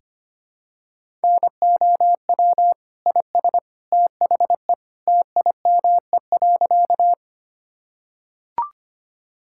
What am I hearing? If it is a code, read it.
Morse code, heard as a keyed tone, about 25 wpm, 700 Hz. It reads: NOW IS THE TIME.
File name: practice / 25wpm / Sentences with 2-Word Phrases - 25wpm